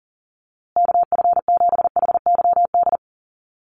Morse code, heard as a keyed tone, about 40 wpm, 700 Hz. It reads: KF7HYD
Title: USA Call Signs Repeated 2x 40wpm